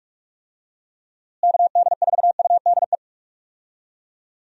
Morse code, heard as a keyed tone, about 43 wpm, 700 Hz. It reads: KD4UDE
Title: USA Call Signs - 6 Characters - Repeated 3x - 45wpm